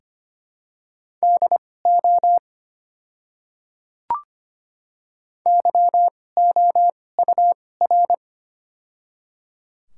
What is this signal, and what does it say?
Morse code, heard as a keyed tone, about 25 wpm, spaced as for 18 wpm, 700 Hz. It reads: DO YOUR
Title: 2x Character Spacing - Top 100 Words - 25wpm